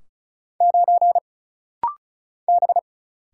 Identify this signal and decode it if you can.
Morse code, heard as a keyed tone, about 36 wpm, 700 Hz. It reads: 9 B
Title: Single Letter-Number Alt - Rapid-Fire 35wpm